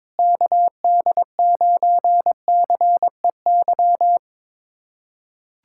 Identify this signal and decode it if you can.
Morse code, heard as a keyed tone, about 22 wpm, 700 Hz. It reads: KD9CEY
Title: USA Callsign DE Callsign 22wpm